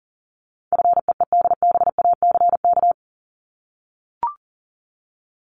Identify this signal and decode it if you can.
Morse code, heard as a keyed tone, about 40 wpm, 700 Hz. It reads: FEEDBACK
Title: Top 1501-1750 Words 40wpm